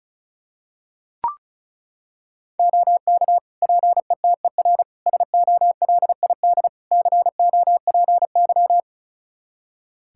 Morse code, heard as a keyed tone, about 34 wpm, 700 Hz. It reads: OK PETER SOLID COPY
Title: K1USN SST QSO Practice 35wpm